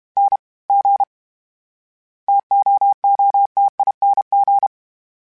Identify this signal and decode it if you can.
Morse code, heard as a keyed tone, about 32 wpm, 800 Hz. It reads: N G TOOTING
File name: Variable Speed / Variable Speed-mp3s / T AOEI's sp @30